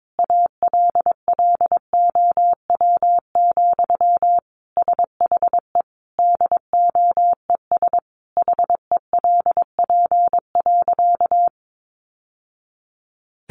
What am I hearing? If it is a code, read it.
Morse code, heard as a keyed tone, about 22 wpm, 700 Hz. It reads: ALLOW, SHE DOES HELP.
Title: Sentences from Top 700 Words 22wpm